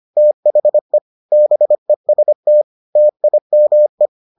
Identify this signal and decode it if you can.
Morse code, heard as a keyed tone, about 25 wpm, 600 Hz. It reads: THE BEST TIME